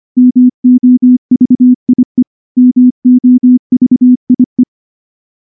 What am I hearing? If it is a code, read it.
Morse code, heard as a keyed tone, about 24 wpm, 250 Hz. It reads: MOVIE MOVIE
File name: morse / morse-training-25WPM